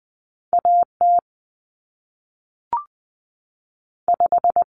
Morse code, heard as a keyed tone, about 20 wpm, 700 Hz. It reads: AT 5
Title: Common Cut Numbers - 2-Digit 20wpm